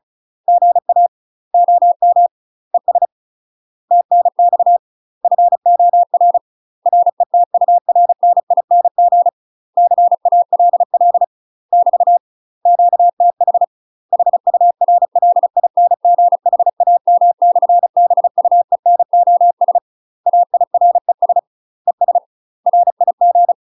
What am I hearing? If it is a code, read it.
Morse code, heard as a keyed tone, about 34 wpm, 700 Hz. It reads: GA OM ES TNX FOR RETURNING CALL <BT> QTH HURLINGHAM/BUENOS AIRES ES RIG
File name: UK MASTER QSOs TRAINING/35WPM